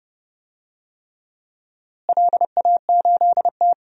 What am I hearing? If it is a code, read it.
Morse code, heard as a keyed tone, about 30 wpm, 700 Hz. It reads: LA8T
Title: DXCC-100 Call Signs - 4 Characters - Repeated 3x - 30wpm